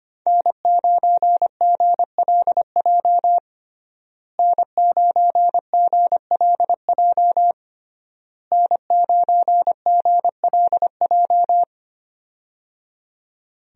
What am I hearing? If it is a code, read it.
Morse code, heard as a keyed tone, about 25 wpm, 700 Hz. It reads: N9GLJ N9GLJ N9GLJ
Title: USA Call Signs - 5 Characters - Repeated 3x - 25wpm